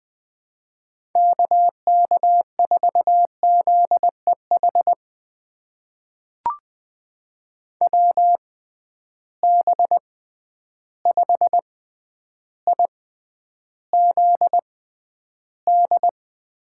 Morse code, heard as a keyed tone, about 20 wpm, spaced as for 10 wpm, 700 Hz. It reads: KK4ZEH WB5IZD
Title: USA Call Signs - 6 Characters - 6x Character Spacing - 20wpm